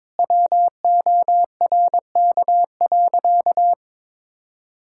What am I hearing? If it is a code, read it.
Morse code, heard as a keyed tone, about 22 wpm, 700 Hz. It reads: WORK.
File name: practice / 22wpm / Sentences with 2-Word Phrases - 22wpm